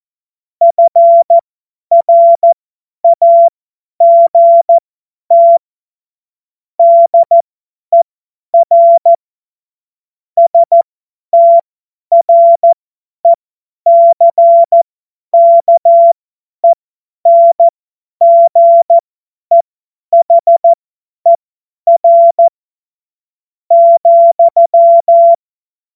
Morse code, heard as a keyed tone, about 14 wpm, spaced as for 10 wpm, 680 Hz. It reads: FRAGT DER STRECKENGEHER ,
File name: Morsekurs / Witze / Witz 01. August 2023 50 F70